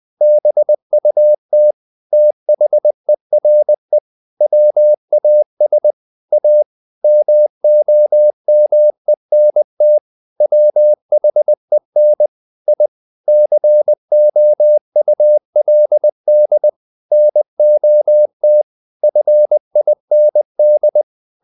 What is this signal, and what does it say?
Morse code, heard as a keyed tone, about 20 wpm, 600 Hz. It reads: BUT THERE WAS A MOMENT WHEN I COULD NOT FIND
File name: the old man and the sea part 2 20wpm